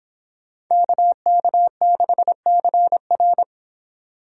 Morse code, heard as a keyed tone, about 26 wpm, 700 Hz. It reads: KK6CR